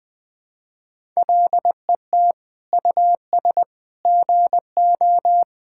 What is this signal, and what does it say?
Morse code, heard as a keyed tone, about 20 wpm, 700 Hz. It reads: LET US GO
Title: Sentences from Top 1000 Words - Review Part 5 - 20wpm